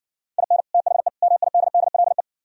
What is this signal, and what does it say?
Morse code, heard as a keyed tone, about 60 wpm, 700 Hz. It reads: IN THE MIDDLE